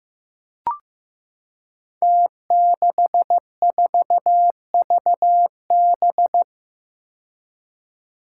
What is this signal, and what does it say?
Morse code, heard as a keyed tone, about 15 wpm, 700 Hz. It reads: T64VB
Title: Five Challenging Letter-Number Combinations 15wpm